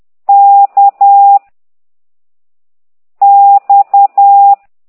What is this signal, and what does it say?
Morse code, heard as a keyed tone, about 10 wpm, 800 Hz. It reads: K X
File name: exerc7